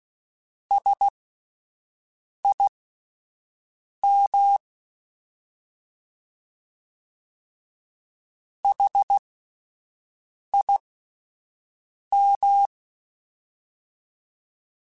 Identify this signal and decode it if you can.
Morse code, heard as a keyed tone, about 16 wpm, spaced as for 5 wpm, 800 Hz. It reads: SIM HIM
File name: S vs H / S vs H mp3s / S vs H @5